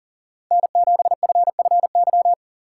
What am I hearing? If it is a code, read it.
Morse code, heard as a keyed tone, about 40 wpm, 700 Hz. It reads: N7FFY
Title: USA Call Signs - 5 Characters - 3x Character Spacing - 40wpm